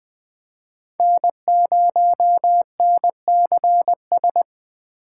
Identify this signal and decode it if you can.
Morse code, heard as a keyed tone, about 20 wpm, 700 Hz. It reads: N0NCS
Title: USA Call Signs - 5 Characters - 2x Character Spacing - 20wpm